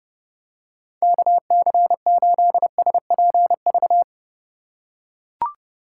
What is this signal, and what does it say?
Morse code, heard as a keyed tone, about 30 wpm, 700 Hz. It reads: KC8SPV
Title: USA Call Signs - 6 Characters - 2x Character Spacing - 30wpm